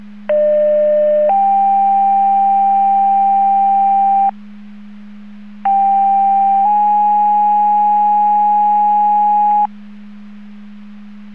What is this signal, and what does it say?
Two-tone page, heard as a one-second tone, then a three-second tone, about 600 then 790 Hz, about 790 then 830 Hz. It is heard from 0.3 to 9.7 s.